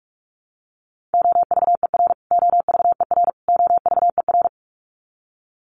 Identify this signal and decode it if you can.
Morse code, heard as a keyed tone, about 44 wpm, 700 Hz. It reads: OVER OVER OVER
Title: Top 300 Words Repeated 45wpm